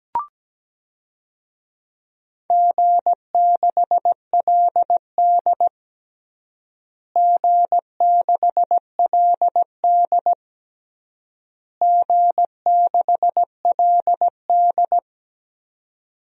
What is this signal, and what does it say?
Morse code, heard as a keyed tone, about 17 wpm, 700 Hz. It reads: G6LD G6LD G6LD
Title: DXCC-100 Call Signs - 4 Characters - Repeated 3x - 17wpm